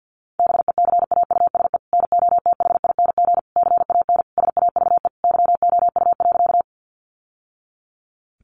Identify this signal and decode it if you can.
Morse code, heard as a keyed tone, about 50 wpm, 700 Hz. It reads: BECAUSE NOTHING CAN SAVE YOU.
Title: Sentences from Top 1100 Words 50wpm